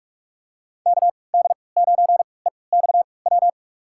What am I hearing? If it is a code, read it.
Morse code, heard as a keyed tone, about 45 wpm, spaced as for 25 wpm, 700 Hz. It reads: KD9EXW
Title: USA Call Signs - 6 Characters - 3x Character Spacing - 45wpm